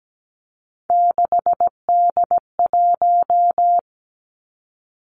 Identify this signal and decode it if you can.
Morse code, heard as a keyed tone, about 17 wpm, 700 Hz. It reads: 6D1
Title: Three Challenging Letter-Number Combinations 17wpm